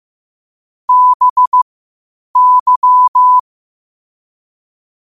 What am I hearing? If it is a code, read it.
Morse code, heard as a keyed tone, about 15 wpm, spaced as for 8 wpm, 1000 Hz. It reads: BY
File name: qotdfiles / 8wpm